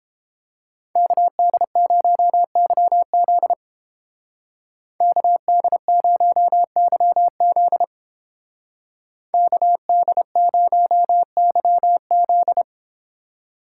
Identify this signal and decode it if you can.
Morse code, heard as a keyed tone, about 30 wpm, 700 Hz. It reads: KD0YZ KD0YZ KD0YZ